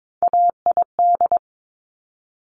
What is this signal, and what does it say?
Morse code, heard as a keyed tone, about 22 wpm, 700 Hz. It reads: AID
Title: Top Three Letter Combinations in Words 22wpm